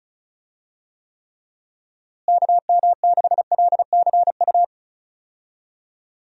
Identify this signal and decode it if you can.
Morse code, heard as a keyed tone, about 36 wpm, 700 Hz. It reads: KM6LCU
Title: USA Call Signs - 6 Characters - Repeated 3x - 35wpm